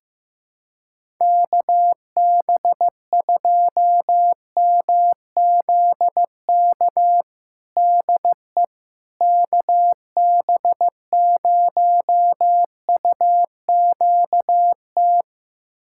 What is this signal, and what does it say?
Morse code, heard as a keyed tone, about 15 wpm, 700 Hz. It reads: KB2MZK DE KB0UQT
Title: USA Callsign DE Callsign 15wpm